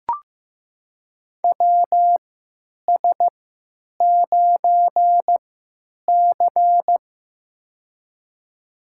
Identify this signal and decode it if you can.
Morse code, heard as a keyed tone, about 15 wpm, spaced as for 9 wpm, 700 Hz. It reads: WS9C
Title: USA Call Signs - 4 Characters - 3x Character Spacing - 15wpm